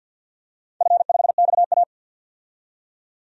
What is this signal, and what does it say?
Morse code, heard as a keyed tone, about 50 wpm, 700 Hz. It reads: F5XA
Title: DXCC-100 Call Signs - 4 Characters - 50wpm